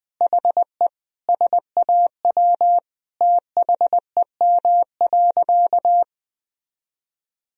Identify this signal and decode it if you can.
Morse code, heard as a keyed tone, about 19 wpm, 700 Hz. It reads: HE SAW THEM.
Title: Sentences from Top 1000 Words - Review Part 1 - 20wpm